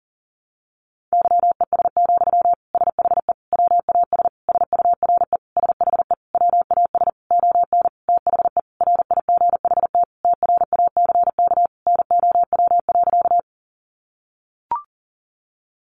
Morse code, heard as a keyed tone, about 40 wpm, 700 Hz. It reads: YES, SHE WAS SURE SHE WAS ON THE RIGHT TRACK NOW.
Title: Sentences from Top 1100 Words 40wpm